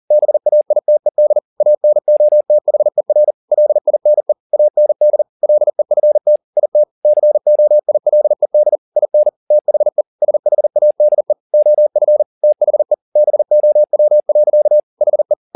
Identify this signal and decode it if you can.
Morse code, heard as a keyed tone, about 40 wpm, 600 Hz. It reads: BAITED ANOTHER LINE AND LEFT IT COILED IN THE SHADE OF THE BOW. HE